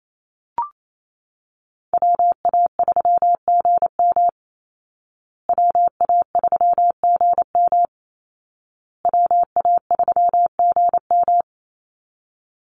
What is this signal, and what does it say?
Morse code, heard as a keyed tone, about 28 wpm, 700 Hz. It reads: WA3GM WA3GM WA3GM
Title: Call Signs Repeated 2x - 28wpm